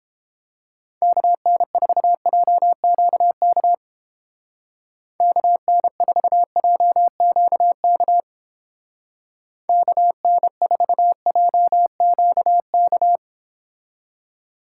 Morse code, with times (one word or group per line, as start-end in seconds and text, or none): 1.0-3.7 KN4JQK
5.2-8.2 KN4JQK
9.7-13.2 KN4JQK